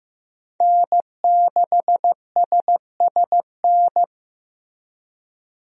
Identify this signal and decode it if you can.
Morse code, heard as a keyed tone, about 15 wpm, 700 Hz. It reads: N6SSN
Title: USA Call Signs - 5 Characters - Repeated 3x - 15wpm